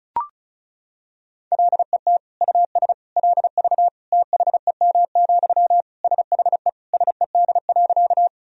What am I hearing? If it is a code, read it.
Morse code, heard as a keyed tone, about 36 wpm, 700 Hz. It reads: LET US LV THEM, SHE SED.
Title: Sentences from Top 400 Words and Abbreviations 35wpm